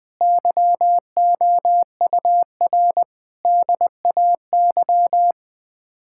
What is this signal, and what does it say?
Morse code, heard as a keyed tone, about 21 wpm, 700 Hz. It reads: YOUR DAY
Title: Sets of 9 Words - 20wpm